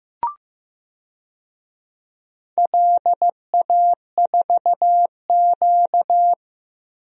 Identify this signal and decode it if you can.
Morse code, heard as a keyed tone, about 15 wpm, 700 Hz. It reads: LA4Q
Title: DXCC-100 Call Signs - 4 Characters - Repeated 2x - 15wpm